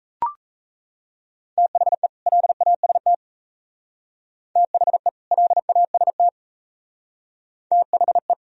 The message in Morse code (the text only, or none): THE LAST THE LAST THE